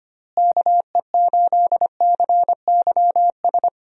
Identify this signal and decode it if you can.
Morse code, heard as a keyed tone, about 25 wpm, 700 Hz. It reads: KE8CYS
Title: USA Call Signs - 6 Characters - Repeated 2x - 25wpm